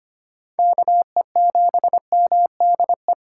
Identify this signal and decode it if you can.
Morse code, heard as a keyed tone, about 25 wpm, 690 Hz. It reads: KE7MDE
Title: USA Call Signs - 4x Character Spacing - 25wpm